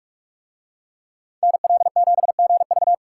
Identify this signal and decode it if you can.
Morse code, heard as a keyed tone, about 43 wpm, 700 Hz. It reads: NL7GV